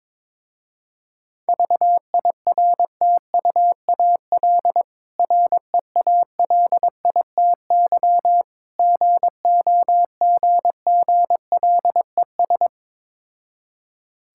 Morse code, heard as a keyed tone, about 23 wpm, 690 Hz. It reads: VIRTUAL REALITY GOGGLES
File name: Difficult Words with Hints - 22wpm